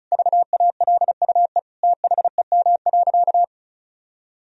Morse code, heard as a keyed tone, about 34 wpm, 700 Hz. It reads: VALUE THEM.
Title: Sentences from Top 1000 Words - Review Part 4 - 35wpm